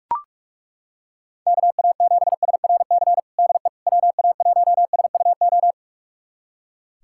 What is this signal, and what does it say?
Morse code, heard as a keyed tone, about 42 wpm, 690 Hz. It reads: KA7SRC DE WA1SUO
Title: USA Callsign DE Callsign - 6 Characters - 45wpm